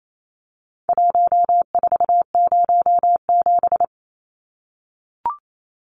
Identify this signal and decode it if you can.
Morse code, heard as a keyed tone, about 28 wpm, 700 Hz. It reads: 1407
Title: Numbers - Contest Style - 4-Digit 28wpm